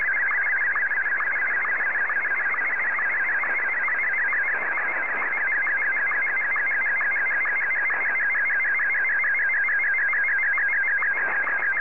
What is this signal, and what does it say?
SSTV (slow-scan television), Martin 1.
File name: M1-VK3